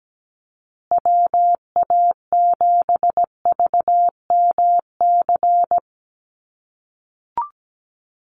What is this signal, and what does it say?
Morse code, heard as a keyed tone, about 17 wpm, 700 Hz. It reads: WA7VMC
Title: USA Call Signs Repeated 3x 17wpm